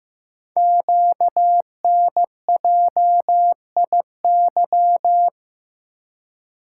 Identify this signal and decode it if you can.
Morse code, heard as a keyed tone, about 15 wpm, 700 Hz. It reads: QNJIY
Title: Code Groups - 5 Letters - 15wpm